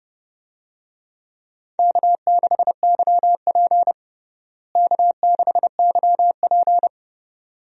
Morse code, heard as a keyed tone, about 32 wpm, 700 Hz. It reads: K6YP K6YP
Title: USA Call Signs - 4 Characters - Repeated 2x - 30wpm